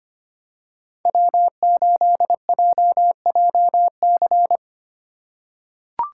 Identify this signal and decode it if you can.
Morse code, heard as a keyed tone, about 25 wpm, 700 Hz. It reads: W8JJC
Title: USA Call Signs - 5 Characters - 3x Character Spacing - 25wpm